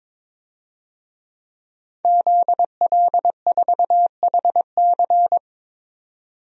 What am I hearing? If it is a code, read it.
Morse code, heard as a keyed tone, about 22 wpm, 700 Hz. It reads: ZL4HC
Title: DXCC-100 Call Signs - 5 Characters - 22wpm